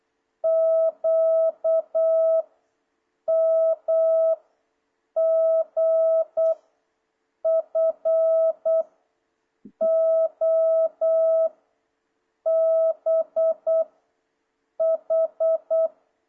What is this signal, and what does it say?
Morse code, heard as a keyed tone, about 8 wpm, 660 Hz. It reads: Q M G F O B H
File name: jh8cbh / 2026cw